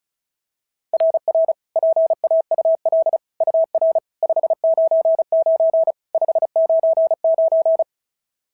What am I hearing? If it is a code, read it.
Morse code, heard as a keyed tone, about 31 wpm, 660 Hz. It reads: RR PAUL UR 599 599
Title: SOTA QSO Practice 35wpm